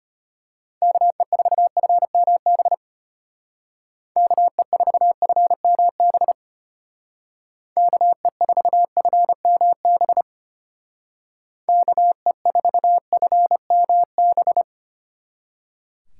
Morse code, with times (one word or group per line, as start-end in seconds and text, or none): 0.8-2.7 KE4FMB
4.2-6.3 KE4FMB
7.8-10.2 KE4FMB
11.7-14.6 KE4FMB